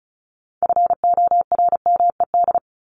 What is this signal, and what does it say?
Morse code, heard as a keyed tone, about 34 wpm, 700 Hz. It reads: FORMED
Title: ED Suffix - Top 100 Words with Base Words 35wpm